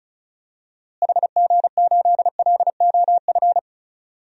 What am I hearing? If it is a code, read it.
Morse code, heard as a keyed tone, about 36 wpm, 700 Hz. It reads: HG8LOF
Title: DXCC-100 Call Signs - 6 Characters - Repeated 2x - 35wpm